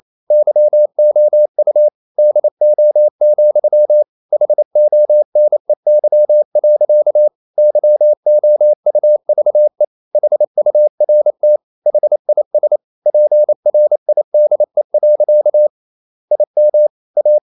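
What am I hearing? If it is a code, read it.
Morse code, heard as a keyed tone, about 28 wpm, 600 Hz. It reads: YOU DO, HONEY. YOUVE HURT HIS PRIDE. IM A
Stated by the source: Morse code practice at 28 wpm, text of Ken Follett's novel Never in English, from CW med Gnister.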